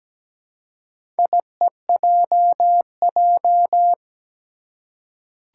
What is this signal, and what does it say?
Morse code, heard as a keyed tone, about 17 wpm, 700 Hz. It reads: IEJJ
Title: Four Challenging Letter-Number Combinations 17wpm